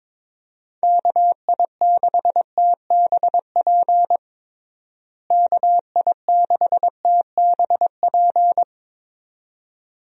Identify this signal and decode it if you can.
Morse code, heard as a keyed tone, about 22 wpm, 700 Hz. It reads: KI6TBP KI6TBP
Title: USA Call Signs - 6 Characters - Repeated 2x - 22wpm